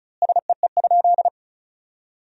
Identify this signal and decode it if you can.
Morse code, heard as a keyed tone, about 36 wpm, 700 Hz. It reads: SEE?
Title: 3x Word-Spacing - Sentences from Top 100 Words - 35wpm